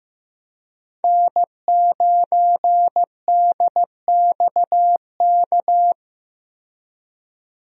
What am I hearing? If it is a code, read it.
Morse code, heard as a keyed tone, about 15 wpm, 700 Hz. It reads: N9DXK
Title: USA Call Signs - 5 Characters - 15wpm